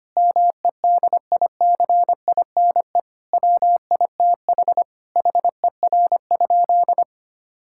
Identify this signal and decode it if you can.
Morse code, heard as a keyed tone, about 25 wpm, 700 Hz. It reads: MEDICINE WITH HER?
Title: Sentences from Top 1000 Words - Review Part 6 - 25wpm